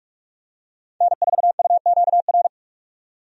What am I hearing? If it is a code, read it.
Morse code, heard as a keyed tone, about 46 wpm, 700 Hz. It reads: N4UQR
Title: USA Call Signs - 5 Characters - 45wpm